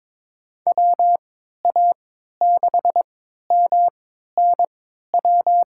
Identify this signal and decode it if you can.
Morse code, heard as a keyed tone, about 22 wpm, spaced as for 13 wpm, 700 Hz. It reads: WA6MNW
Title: USA Call Signs - 3x Character Spacing - 22wpm